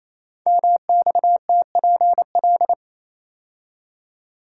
Morse code, heard as a keyed tone, about 28 wpm, 700 Hz. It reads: MXTPL
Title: Code Groups - 5 Letters - 28wpm